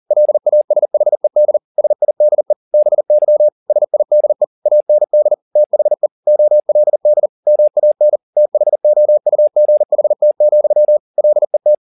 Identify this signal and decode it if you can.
Morse code, heard as a keyed tone, about 40 wpm, 600 Hz. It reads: LASHED SIDE BY SIDE AND THE OLD MAN THOUGHT, LET
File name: the old man and the sea 2 40wpm